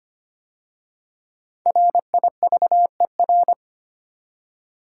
Morse code, heard as a keyed tone, about 25 wpm, 700 Hz. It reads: RIVER
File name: 5 Letter Words - First 3 Letters Given - 25wpm